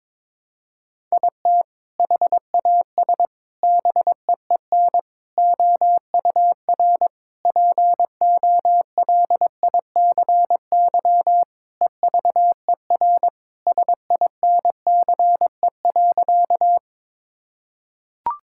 Morse code, heard as a keyed tone, about 22 wpm, 700 Hz. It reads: IT HAS BEEN OUR POLICY EVER SINCE.
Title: Sentences from Top 1000 Words - Review Part 5 - 22wpm